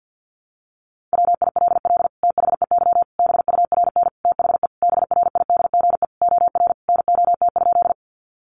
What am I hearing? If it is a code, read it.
Morse code, heard as a keyed tone, about 50 wpm, 700 Hz. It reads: WILL THEY BURN THE BRIDGE OR NOT?
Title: Sentences from Top 1300 Words 50wpm